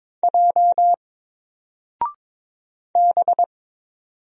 Morse code, heard as a keyed tone, about 22 wpm, 700 Hz. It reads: J B
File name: Single-Letter-Number-22wpm